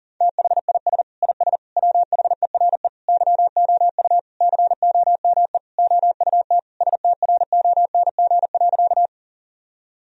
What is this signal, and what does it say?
Morse code, heard as a keyed tone, about 40 wpm, 700 Hz. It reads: THIS IS WHERE YOU COME OUT STRONG.
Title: Sentences from Top 1000 Words - Review Part 3 - 40wpm